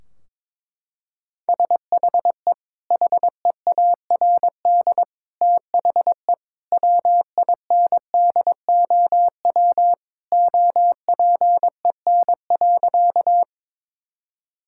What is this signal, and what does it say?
Morse code, heard as a keyed tone, about 22 wpm, 700 Hz. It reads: SHE HEARD THE WINDOW OPEN.
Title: Sentences from Top 1600 Words 22wpm